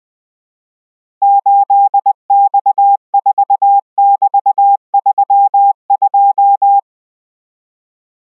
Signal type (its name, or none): Morse code